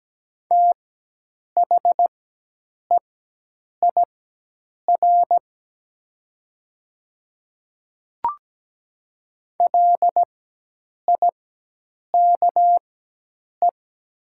Morse code, heard as a keyed tone, about 17 wpm, spaced as for 8 wpm, 700 Hz. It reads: THEIR LIKE
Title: 4x Character Spacing - Top 200 Words - 17wpm